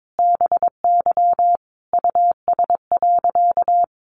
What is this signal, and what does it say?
Morse code, heard as a keyed tone, about 22 wpm, 700 Hz. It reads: BY US.
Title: A Sound Of Thunder 22wpm